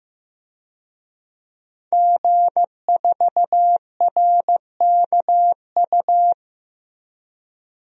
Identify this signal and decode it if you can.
Morse code, heard as a keyed tone, about 15 wpm, 700 Hz. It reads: G4RKU
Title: DXCC-100 Call Signs - 5 Characters - Repeated 2x - 15wpm